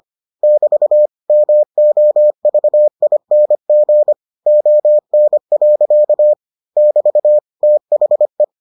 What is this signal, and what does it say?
Morse code, heard as a keyed tone, about 25 wpm, 600 Hz. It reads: <BT> MOVING ON. <BT> THE